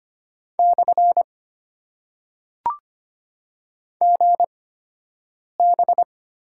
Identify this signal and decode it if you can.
Morse code, heard as a keyed tone, about 25 wpm, 700 Hz. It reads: / G B
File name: Top 90 QSO Elements - 8x Character Spacing - 25wpm